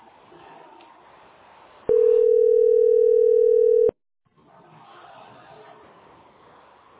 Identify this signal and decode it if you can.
Ringback tone (North America), 440 + 480 Hz.